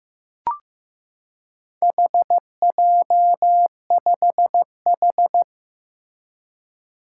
Morse code, heard as a keyed tone, about 15 wpm, 700 Hz. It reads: HJ5H
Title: Four Challenging Letter-Number Combinations 15wpm